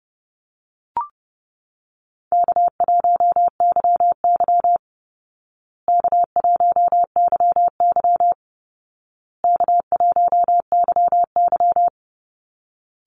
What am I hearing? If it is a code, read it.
Morse code, heard as a keyed tone, about 30 wpm, 700 Hz. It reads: K1YY K1YY K1YY
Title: Easy Call Signs Repeated 3x - 30wpm